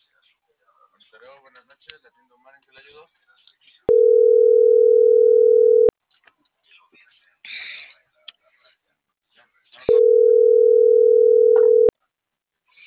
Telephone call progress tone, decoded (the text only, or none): ringback tone